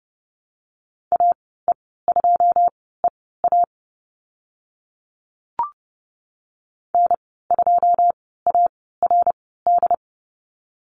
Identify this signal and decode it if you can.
Morse code, heard as a keyed tone, about 30 wpm, spaced as for 17 wpm, 700 Hz. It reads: AE2EA N2ARD
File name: Lesson 14g - Number 2 - Call Signs 30wpm